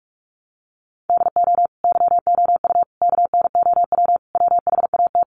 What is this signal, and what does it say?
Morse code, heard as a keyed tone, about 44 wpm, 700 Hz. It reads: DO YOU KNOW WHAT
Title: Sentences from Top 100 Words 45wpm